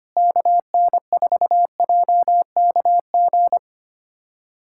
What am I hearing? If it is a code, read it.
Morse code, heard as a keyed tone, about 25 wpm, 700 Hz. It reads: KN4JKG
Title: USA Callsign DE Callsign 25wpm